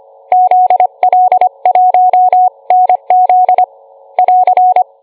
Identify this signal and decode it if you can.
Morse code, heard as a keyed tone, about 25 wpm, 730 Hz. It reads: ZL1NZ <AR>